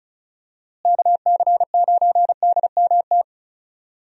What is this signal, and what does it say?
Morse code, heard as a keyed tone, about 36 wpm, 700 Hz. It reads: KC9DMT
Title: USA Call Signs - 6 Characters - Repeated 2x - 35wpm